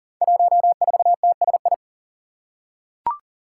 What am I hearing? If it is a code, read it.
Morse code, heard as a keyed tone, about 40 wpm, 700 Hz. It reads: 14TSI
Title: Five Challenging Letter-Number Combinations 40wpm